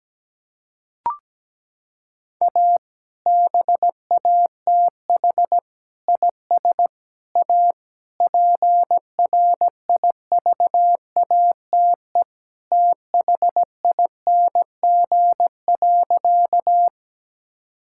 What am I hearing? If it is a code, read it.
Morse code, heard as a keyed tone, about 17 wpm, 700 Hz. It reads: A BATH IS A PRIVATE THING.
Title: Sentences from Top 1600 Words 17wpm